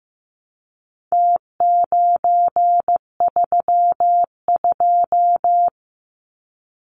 Morse code, heard as a keyed tone, about 15 wpm, 700 Hz. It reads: T932
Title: Numbers - Contest Style - 4-Digit 15wpm